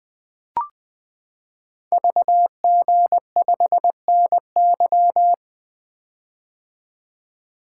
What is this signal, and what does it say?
Morse code, heard as a keyed tone, about 20 wpm, 700 Hz. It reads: VG5NY
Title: Code Groups - 5 Letter:Numbers - 20wpm